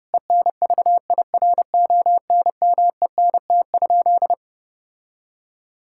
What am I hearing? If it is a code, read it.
Morse code, heard as a keyed tone, about 30 wpm, 700 Hz. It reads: ENVIRONMENT?
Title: Sentences from Top 1000 Words - Review Part 5 - 30wpm